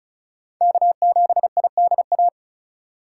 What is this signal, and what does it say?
Morse code, heard as a keyed tone, about 36 wpm, 700 Hz. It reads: K7IDA
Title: USA Call Signs - 5 Characters - 3x Character Spacing - 35wpm